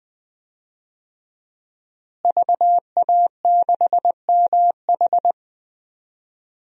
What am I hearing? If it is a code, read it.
Morse code, heard as a keyed tone, about 20 wpm, 700 Hz. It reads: VA6MH
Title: DXCC-100 Call Signs - 5 Characters - Repeated 3x - 20wpm